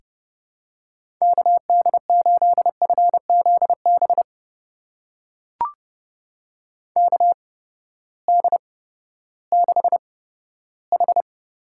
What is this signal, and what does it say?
Morse code, heard as a keyed tone, about 30 wpm, spaced as for 15 wpm, 700 Hz. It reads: KD8FZB KD6H
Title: USA Call Signs - 6 Characters - 8x Character Spacing - 30wpm